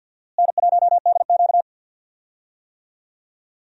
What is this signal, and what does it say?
Morse code, heard as a keyed tone, about 51 wpm, 700 Hz. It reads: N1DQ
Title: USA Call Signs - 4 Characters - Repeated 2x - 50wpm